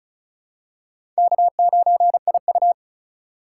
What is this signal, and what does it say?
Morse code, heard as a keyed tone, about 36 wpm, 700 Hz. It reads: K9IU
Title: USA Call Signs - 4 Characters - 4x Character Spacing - 35wpm